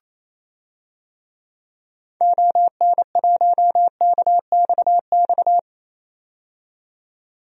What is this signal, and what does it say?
Morse code, heard as a keyed tone, about 28 wpm, 700 Hz. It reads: ON1KXX
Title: DXCC-100 Call Signs - 6 Characters - 28wpm